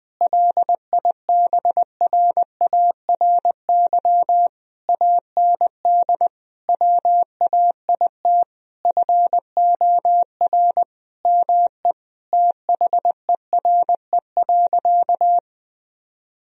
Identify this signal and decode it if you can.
Morse code, heard as a keyed tone, about 20 wpm, 700 Hz. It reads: LIBRARY AND WAIT FOR ME THERE.
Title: Sentences from Top 1000 Words - Review Part 4 - 20wpm